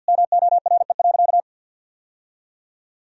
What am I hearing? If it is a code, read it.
Morse code, heard as a keyed tone, about 52 wpm, 700 Hz. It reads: MORE.
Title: Sentences from Top 1000 Words - Review Part 3 - 50wpm